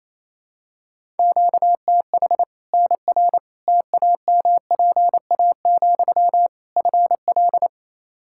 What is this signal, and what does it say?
Morse code, heard as a keyed tone, about 27 wpm, 700 Hz. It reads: QTH NR TAMPA, FL
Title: Practice QSO - Exchange 1 - Beginning - 28wpm